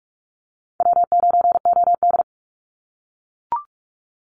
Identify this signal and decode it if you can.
Morse code, heard as a keyed tone, about 44 wpm, 700 Hz. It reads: W9OD
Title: Call Signs - 45wpm